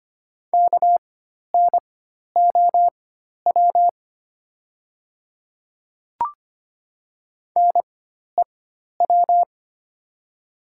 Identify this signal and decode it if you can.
Morse code, heard as a keyed tone, about 25 wpm, spaced as for 12 wpm, 700 Hz. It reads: KNOW NEW
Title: 4x Character Spacing - Top 300 Words - 25wpm